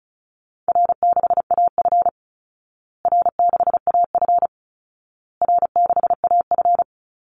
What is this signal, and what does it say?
Morse code, heard as a keyed tone, about 34 wpm, 700 Hz. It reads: R6AF R6AF R6AF
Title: Call Signs Repeated 2x - 35wpm